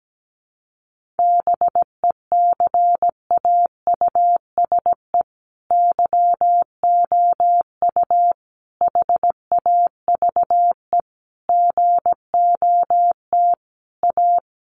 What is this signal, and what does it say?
Morse code, heard as a keyed tone, about 17 wpm, 700 Hz. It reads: BECAUSE YOU HAVE GOT A
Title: Sentences from Top 1500 Words 17wpm